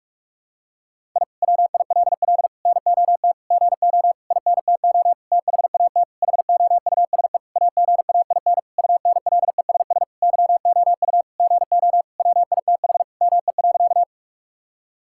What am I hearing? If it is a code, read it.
Morse code, heard as a keyed tone, about 46 wpm, 700 Hz. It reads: I WILL NOT GO INTO THAT HOUSE AGAIN UNLESS YOU GO WITH ME.
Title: Sentences from Top 1000 Words - Review Part 6 - 45wpm